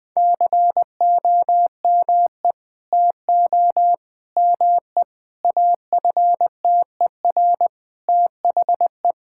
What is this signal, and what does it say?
Morse code, heard as a keyed tone, about 20 wpm, 700 Hz. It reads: COME TO ME AFTER THE
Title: Sentences from Top 1000 Words - Review Part 6 - 20wpm